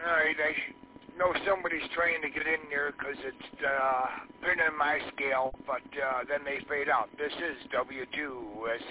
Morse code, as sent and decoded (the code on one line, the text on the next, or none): none